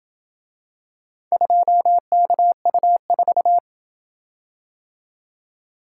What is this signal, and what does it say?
Morse code, heard as a keyed tone, about 27 wpm, 700 Hz. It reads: 2KU4